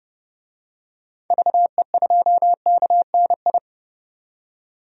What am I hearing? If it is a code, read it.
Morse code, heard as a keyed tone, about 32 wpm, 700 Hz. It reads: VE2KNI
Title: DXCC-100 Call Signs - 6 Characters - 30wpm